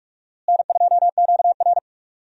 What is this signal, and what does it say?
Morse code, heard as a keyed tone, about 46 wpm, 700 Hz. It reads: N2QR
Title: USA Call Signs - 4 Characters - Repeated 3x - 45wpm